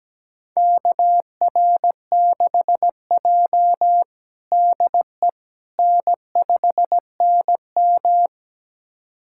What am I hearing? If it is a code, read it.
Morse code, heard as a keyed tone, about 17 wpm, 700 Hz. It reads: KR6J DE N5NM
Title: USA Callsign DE Callsign - 4 Characters - 17wpm